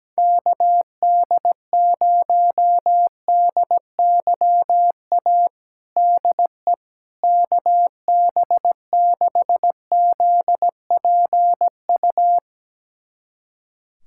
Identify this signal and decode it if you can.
Morse code, heard as a keyed tone, about 17 wpm, 700 Hz. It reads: KD0DYA DE KB6ZPU
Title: USA Callsign DE Callsign 17wpm